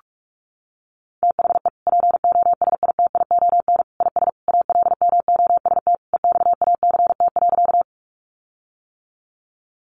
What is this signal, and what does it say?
Morse code, heard as a keyed tone, about 44 wpm, 700 Hz. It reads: THE POSITION IS ALMOST EXACT.
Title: Sentences from Top 1300 Words 45wpm